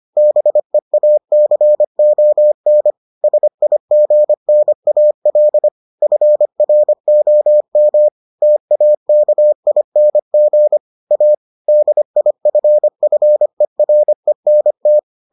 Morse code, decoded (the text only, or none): BEACON SIGNAL FROM TAKING A DIFFERENT